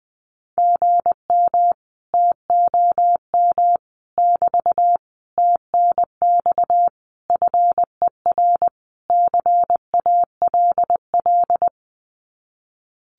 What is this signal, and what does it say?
Morse code, heard as a keyed tone, about 20 wpm, 700 Hz. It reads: GM TOM <BT> TNX FER CALL
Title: Common QSO Phrases Encore 20wpm